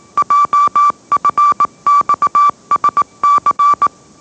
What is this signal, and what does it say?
Morse code, heard as a keyed tone, about 18 wpm, 1160 Hz. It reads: JFXSC
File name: mrx / mrx082 20wpm